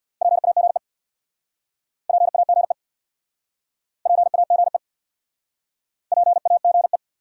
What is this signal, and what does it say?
Morse code, heard as a keyed tone, about 69 wpm, 700 Hz. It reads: PAGE PAGE PAGE PAGE